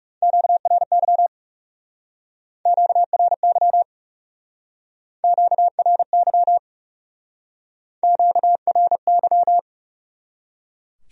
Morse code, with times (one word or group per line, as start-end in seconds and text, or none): 0.2-1.3 QRY
2.7-3.8 QRY
5.2-6.6 QRY
8.0-9.6 QRY